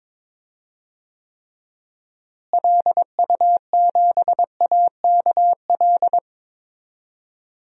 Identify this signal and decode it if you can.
Morse code, heard as a keyed tone, about 22 wpm, 700 Hz. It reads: LU7AKL
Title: DXCC-100 Call Signs - 6 Characters - Repeated 3x - 22wpm